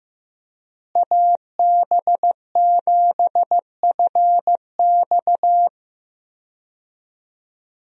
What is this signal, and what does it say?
Morse code, heard as a keyed tone, about 15 wpm, 700 Hz. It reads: AB7FX
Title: USA Call Signs - 5 Characters - Repeated 3x - 15wpm